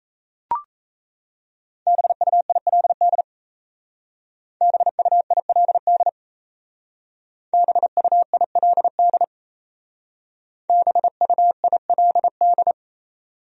Morse code, heard as a keyed tone, about 35 wpm, 700 Hz. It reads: BUILD BUILD BUILD BUILD